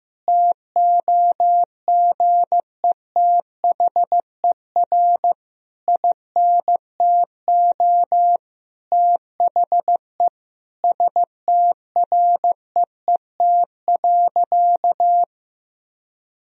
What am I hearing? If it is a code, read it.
Morse code, heard as a keyed tone, about 15 wpm, 700 Hz. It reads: TOGETHER INTO THE STREET.
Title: Sentences from Top 1000 Words - Review Part 3 - 15wpm